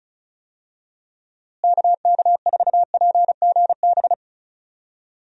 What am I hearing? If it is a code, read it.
Morse code, heard as a keyed tone, about 34 wpm, 700 Hz. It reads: KK4PGB
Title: USA Call Signs - 6 Characters - Repeated 2x - 35wpm